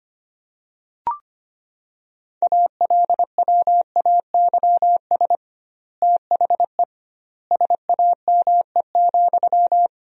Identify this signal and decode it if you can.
Morse code, heard as a keyed tone, about 25 wpm, 700 Hz. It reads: ALWAYS THE SAME,
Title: 2x Word-Spacing - Sentences from Top 300 Words - 25wpm